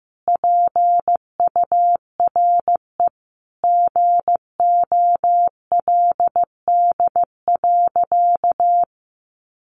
Morse code, heard as a keyed tone, about 15 wpm, 700 Hz. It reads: PURE GOLD.
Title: Sentences from Top 1500 Words 15wpm